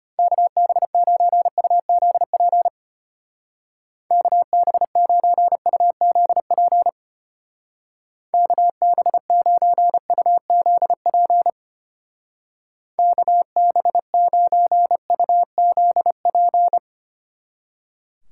Morse code, heard as a keyed tone, about 31 wpm, 700 Hz. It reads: KB9UZP KB9UZP KB9UZP KB9UZP
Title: Speed Racing - USA 2x3 Call Signs 25wpm